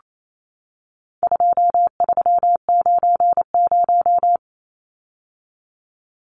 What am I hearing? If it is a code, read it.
Morse code, heard as a keyed tone, about 28 wpm, 700 Hz. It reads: 2390
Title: Numbers - 4-Digit 28wpm